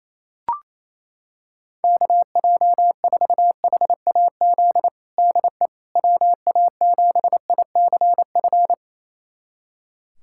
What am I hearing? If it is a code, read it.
Morse code, heard as a keyed tone, about 28 wpm, 700 Hz. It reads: KJ4HAZ DE WA7ICF
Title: USA Callsign DE Callsign - 6 Characters - 28wpm